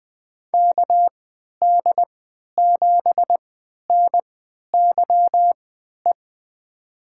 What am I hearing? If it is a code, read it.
Morse code, heard as a keyed tone, about 21 wpm, spaced as for 11 wpm, 700 Hz. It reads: KD7NYE
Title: USA Call Signs - 3x Character Spacing - 20wpm